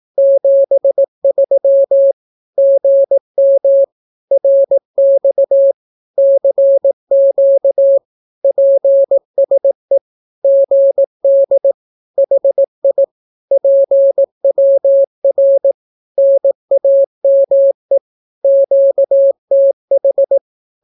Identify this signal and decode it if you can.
Morse code, heard as a keyed tone, about 18 wpm, 550 Hz. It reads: 73 GM RX CQ PSE GD HI PWR NAME QTH
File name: naukapodstaw / skroty / skroty